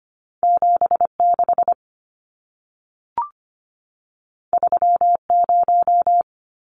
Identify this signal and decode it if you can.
Morse code, heard as a keyed tone, about 25 wpm, 700 Hz. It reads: 76 30
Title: Numbers - 2-Digit 25wpm